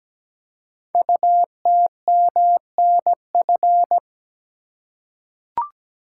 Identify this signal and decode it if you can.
Morse code, heard as a keyed tone, about 17 wpm, 700 Hz. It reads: UTMNF
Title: Code Groups - 5 Letters - 17wpm